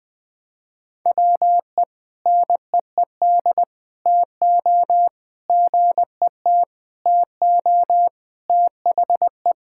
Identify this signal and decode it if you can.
Morse code, heard as a keyed tone, about 20 wpm, 700 Hz. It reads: WE NEED TO GET TO THE
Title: Sentences with 7-Word Phrases - 20wpm